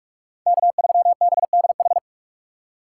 Morse code, heard as a keyed tone, about 42 wpm, 700 Hz. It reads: K3BDH
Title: USA Call Signs - 5 Characters - 4x Character Spacing - 45wpm